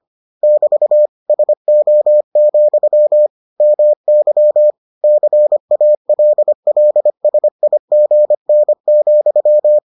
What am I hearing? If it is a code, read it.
Morse code, heard as a keyed tone, about 25 wpm, 600 Hz. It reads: <BT> SO, MY CALLSIGN,